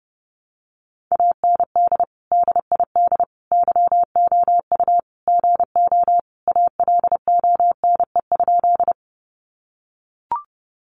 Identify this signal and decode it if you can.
Morse code, heard as a keyed tone, about 30 wpm, 700 Hz. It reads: AND DID YOU GO ALONE?
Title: Sentences from Top 1000 Words 30wpm